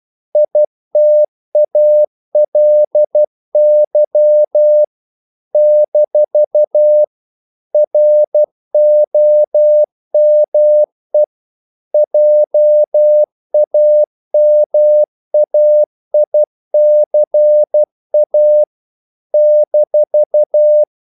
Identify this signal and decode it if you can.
Morse code, heard as a keyed tone, about 12 wpm, 600 Hz. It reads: ITALY - ROME JAMAICA -
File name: CW / Practice / NationalCapitals12